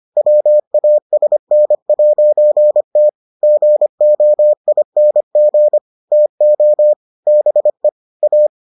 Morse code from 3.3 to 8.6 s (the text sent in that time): GOING TO BE A